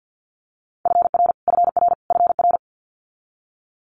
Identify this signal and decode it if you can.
Morse code, heard as a keyed tone, about 50 wpm, 690 Hz. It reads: FR FR FR
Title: Multi-letter Phonemes Repeated 3x 50wpm